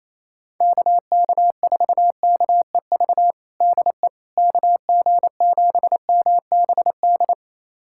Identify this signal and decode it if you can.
Morse code, heard as a keyed tone, about 28 wpm, 700 Hz. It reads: KK4KEV DE KG7MBD
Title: USA Callsign DE Callsign - 6 Characters - 28wpm